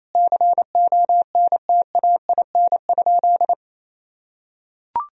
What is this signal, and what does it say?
Morse code, heard as a keyed tone, about 28 wpm, 700 Hz. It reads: CONTAIN?
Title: Sentences from Top 1000 Words - Review Part 2 - 28wpm